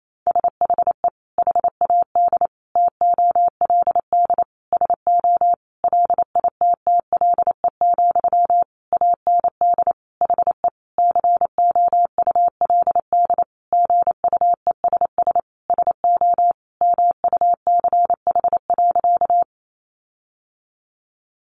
Morse code, heard as a keyed tone, about 28 wpm, 700 Hz. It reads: SHE HAD TOLD SO LITTLE, AND HE COULD GUESS SO MUCH.
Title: Sentences from Top 1500 Words 28wpm